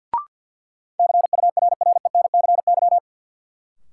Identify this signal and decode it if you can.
Morse code, heard as a keyed tone, about 52 wpm, 700 Hz. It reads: CURRENCY
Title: Sentences from Top 1500 Words - New Word Drill - 50wpm